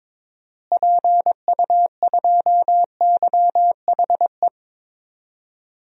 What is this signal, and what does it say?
Morse code, heard as a keyed tone, about 22 wpm, 700 Hz. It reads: PU2YHE
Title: DXCC-100 Call Signs - 6 Characters - Repeated 2x - 22wpm